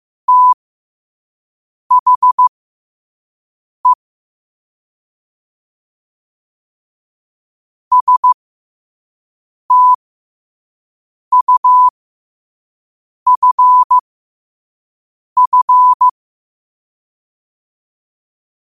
Morse code, heard as a keyed tone, about 15 wpm, spaced as for 5 wpm, 1000 Hz. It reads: THE STUFF
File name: qotdfiles / 5wpm